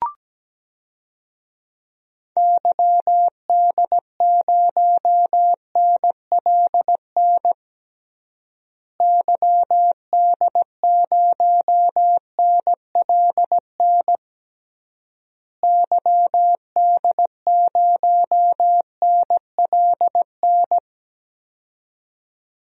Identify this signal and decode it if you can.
Morse code, heard as a keyed tone, about 17 wpm, 700 Hz. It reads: YD0NLN YD0NLN YD0NLN
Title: DXCC-100 Call Signs - 6 Characters - Repeated 3x - 17wpm